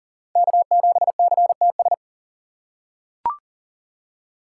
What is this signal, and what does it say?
Morse code, heard as a keyed tone, about 40 wpm, 700 Hz. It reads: K7CTS
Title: USA Call Signs - 5 Characters - 4x Character Spacing - 40wpm